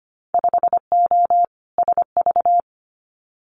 Morse code, heard as a keyed tone, about 25 wpm, 700 Hz. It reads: 5O SV
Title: Winter Field Day Exchanges Encore 25wpm